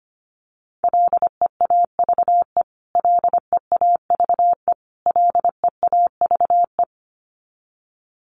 Morse code, heard as a keyed tone, about 25 wpm, 700 Hz. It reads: LEAVE LEAVE LEAVE
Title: Top 500 Words Repeated 25wpm